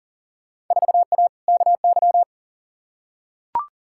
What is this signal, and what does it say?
Morse code, heard as a keyed tone, about 40 wpm, 700 Hz. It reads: 4A KY
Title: Field Day Exchanges Encore - 40wpm